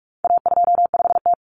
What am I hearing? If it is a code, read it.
Morse code, heard as a keyed tone, about 44 wpm, 700 Hz. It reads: A25T
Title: Common Cut Numbers - 4-Digit 45wpm